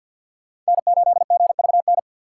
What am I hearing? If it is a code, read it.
Morse code, heard as a keyed tone, about 47 wpm, 700 Hz. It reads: N8GVN
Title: USA Call Signs - 5 Characters - 3x Character Spacing - 50wpm